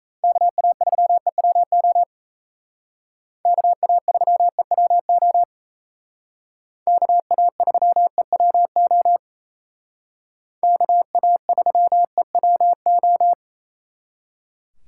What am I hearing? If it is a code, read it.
Morse code, heard as a keyed tone, about 37 wpm, 700 Hz. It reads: KA3EWO KA3EWO KA3EWO KA3EWO